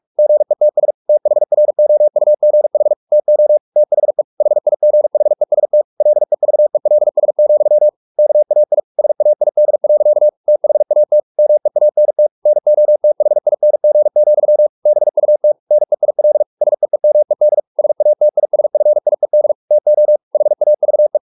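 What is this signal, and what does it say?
Morse code, heard as a keyed tone, about 44 wpm, 600 Hz. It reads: GETS THROUGH TO THE HIGHEST LEVELS, KAI SAID. THAT MEANT NOTHING, BUT NEIL SEEMED SATISFIED TO HAVE
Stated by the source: Morse practice at 45 wpm, the novel Never by Ken Follett in English, from CW med Gnister.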